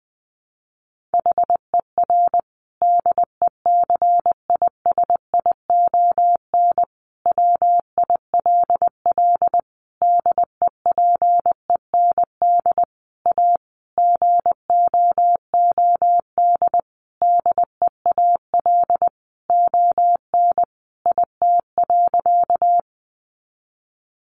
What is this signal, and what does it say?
Morse code, heard as a keyed tone, about 19 wpm, 700 Hz. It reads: HER DECISION WILL DEPEND A GOOD DEAL ON IT.
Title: Sentences from Top 1300 Words 20wpm